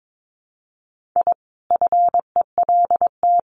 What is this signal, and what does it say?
Morse code, heard as a keyed tone, about 22 wpm, 700 Hz. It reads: I FELT
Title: Sets of 5 Words Encore 22wpm